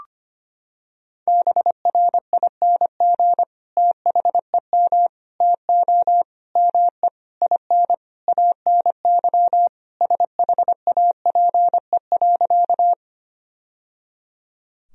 Morse code, heard as a keyed tone, about 25 wpm, 700 Hz. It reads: BRING THEM TO ME IN ANY SHAPE.
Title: Sentences from Top 1000 Words - Review Part 3 - 25wpm